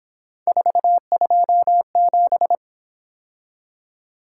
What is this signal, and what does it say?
Morse code, heard as a keyed tone, about 26 wpm, 700 Hz. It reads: 427
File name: Speed-Racing - Three Letter-Number Combinations - 22wpm